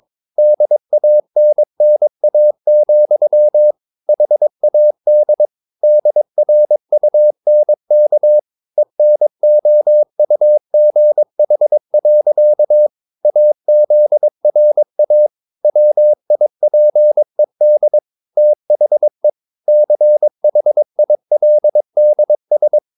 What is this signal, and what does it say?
Morse code, heard as a keyed tone, about 22 wpm, 600 Hz. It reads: DANNA, HAD DRUNK ENOUGH. AZRA WIPED THE CHILDS
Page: Romanen Never af Ken Follet på engelsk.